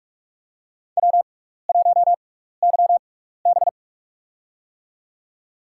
Morse code, heard as a keyed tone, about 47 wpm, spaced as for 16 wpm, 700 Hz. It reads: W1YB
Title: USA Call Signs - 4 Characters - 6x Character Spacing - 45wpm